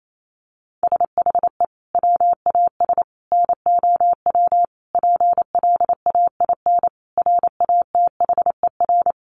Morse code, heard as a keyed tone, about 28 wpm, 700 Hz. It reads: SHE WAS NOW PLAIN RATHER
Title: Sentences from Top 1000 Words 28wpm